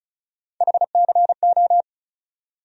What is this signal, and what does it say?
Morse code, heard as a keyed tone, about 36 wpm, 700 Hz. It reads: HCO
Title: USA Call Sign Postfixes - 3 Letters 35wpm